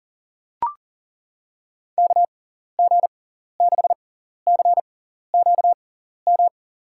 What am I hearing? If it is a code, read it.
Morse code, heard as a keyed tone, about 40 wpm, spaced as for 14 wpm, 700 Hz. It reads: KG6CQM
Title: USA Call Signs - 6x Character Spacing - 40wpm